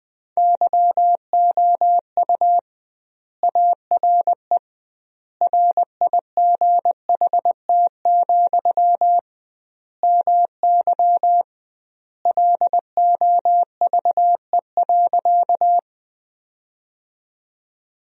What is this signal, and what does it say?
Morse code, heard as a keyed tone, about 20 wpm, 700 Hz. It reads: YOU ARE RIGHT, MY LOVE.
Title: 2x Word-Spacing - Sentences from Top 400 Words - 20wpm